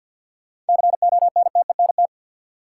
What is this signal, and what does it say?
Morse code, heard as a keyed tone, about 52 wpm, 700 Hz. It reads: CONTENT
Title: Sentences from Top 900 Words - New Word Drill - 50wpm